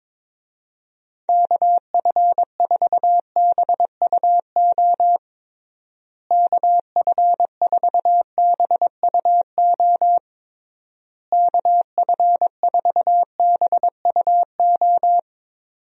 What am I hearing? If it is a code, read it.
Morse code, heard as a keyed tone, about 22 wpm, 700 Hz. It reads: KF4BUO KF4BUO KF4BUO
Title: USA Call Signs - 6 Characters - Repeated 3x - 22wpm